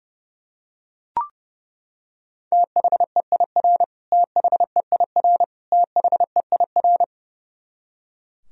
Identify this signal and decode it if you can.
Morse code, heard as a keyed tone, about 30 wpm, 700 Hz. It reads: THEIR THEIR THEIR
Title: Top 100 Words Repeated 30wpm